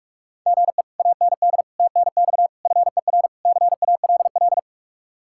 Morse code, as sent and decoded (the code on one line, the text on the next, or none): --. / .- -. -.. / - -. -..- / ..-. .-. / -.-. .- .-.. .-..
GE AND TNX FER CALL